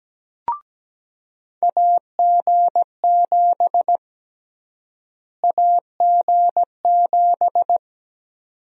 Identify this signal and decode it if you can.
Morse code, heard as a keyed tone, about 17 wpm, 700 Hz. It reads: AG7 AG7
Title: USA Call Sign Prefixes - Repeated 2x - 17wpm